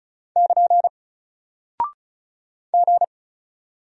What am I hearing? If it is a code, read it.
Morse code, heard as a keyed tone, about 36 wpm, 700 Hz. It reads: <KN> G
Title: Single Character (Letter:Number:Punctuation:Prosign) Alt 35wpm